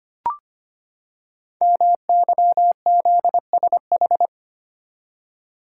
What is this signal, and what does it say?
Morse code, heard as a keyed tone, about 25 wpm, 700 Hz. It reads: MYZSH
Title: Code Groups - 5 Letter:Numbers - 25wpm